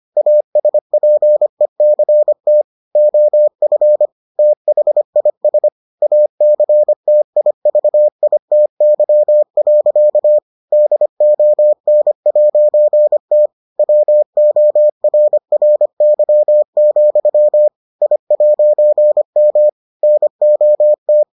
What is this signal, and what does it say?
Morse code, heard as a keyed tone, about 25 wpm, 600 Hz. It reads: ASPECT OF THIS ACTIVITY. DON'T WORRY, I'M NOT